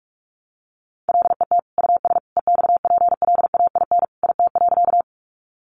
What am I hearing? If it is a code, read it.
Morse code, heard as a keyed tone, about 44 wpm, 700 Hz. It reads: LET US EXPLAIN IT.
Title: Sentences from Top 1100 Words 45wpm